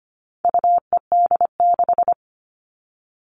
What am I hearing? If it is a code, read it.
Morse code, heard as a keyed tone, about 25 wpm, 700 Hz. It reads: UED6
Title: All Cut Numbers - 4-Digit 25wpm